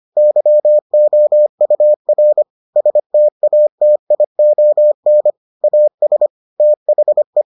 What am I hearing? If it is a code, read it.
Morse code, heard as a keyed tone, about 25 wpm, 600 Hz. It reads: YOUR STATION AS THE